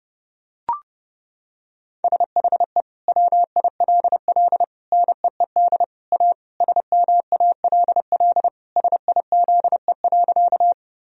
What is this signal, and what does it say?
Morse code, heard as a keyed tone, about 30 wpm, 700 Hz. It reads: SHE WILL NEED A SMALL SIZE.
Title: Sentences from Top 1000 Words - Review Part 2 - 30wpm